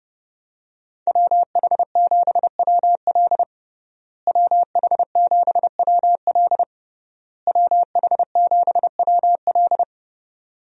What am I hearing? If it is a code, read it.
Morse code, heard as a keyed tone, about 30 wpm, 700 Hz. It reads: WH7WL WH7WL WH7WL
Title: USA Call Signs - 5 Characters - Repeated 3x - 30wpm